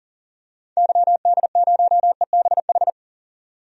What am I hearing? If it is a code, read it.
Morse code, heard as a keyed tone, about 40 wpm, 700 Hz. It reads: YD0EBH
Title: DXCC-100 Call Signs - 6 Characters - 40wpm